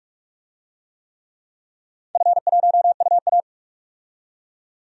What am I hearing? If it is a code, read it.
Morse code, heard as a keyed tone, about 46 wpm, 700 Hz. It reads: F1UA
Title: DXCC-100 Call Signs - 4 Characters - 45wpm